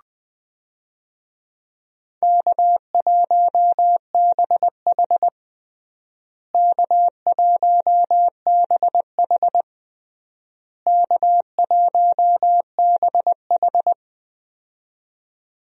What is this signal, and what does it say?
Morse code, heard as a keyed tone, about 20 wpm, 700 Hz. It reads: K1BH K1BH K1BH
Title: USA Call Signs - 4 Characters - Repeated 3x - 20wpm